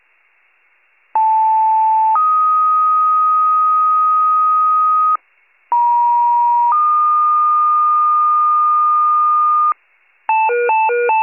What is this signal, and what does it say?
Two-tone page, heard as a one-second tone, then a three-second tone, about 880 then 1250 Hz, about 950 then 1220 Hz. It is heard from 1.1 to 9.7 s.